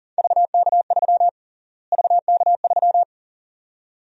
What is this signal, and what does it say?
Morse code, heard as a keyed tone, about 37 wpm, 700 Hz. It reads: VK3 VK3
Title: USA Call Sign Prefixes - Repeated 2x - 40wpm